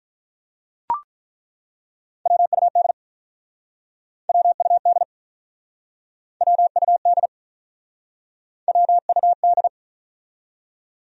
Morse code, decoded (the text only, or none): WUD WUD WUD WUD